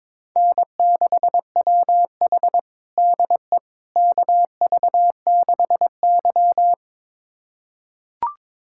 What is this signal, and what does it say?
Morse code, heard as a keyed tone, about 22 wpm, 700 Hz. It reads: N6WH DE KV6Y
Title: USA Callsign DE Callsign - 4 Characters - 22wpm